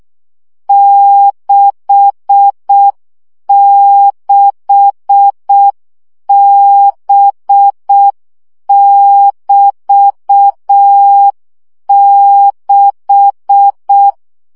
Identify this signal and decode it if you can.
Morse code, heard as a keyed tone, about 6 wpm, 800 Hz. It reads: 66B<BT>6
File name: CW gyakorlas b 6 egyenlo minusz 5 baud